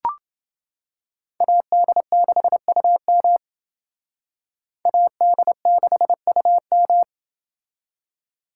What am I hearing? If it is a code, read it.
Morse code, heard as a keyed tone, about 28 wpm, 700 Hz. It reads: AD6UM AD6UM